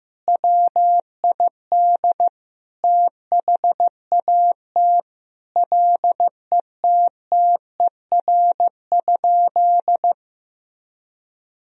Morse code, heard as a keyed tone, about 15 wpm, 700 Hz. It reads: WID THAT LETTER?
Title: Sentences from Top 300 Words and Abbreviations 15wpm